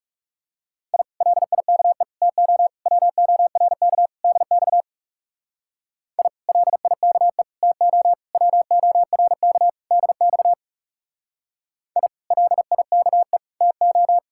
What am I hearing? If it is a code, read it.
Morse code, heard as a keyed tone, about 34 wpm, 700 Hz. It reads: I LIKE TO WORK DX I LIKE TO WORK DX I LIKE TO
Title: Speed-Racing Common QSO Phrases Encore - 30wpm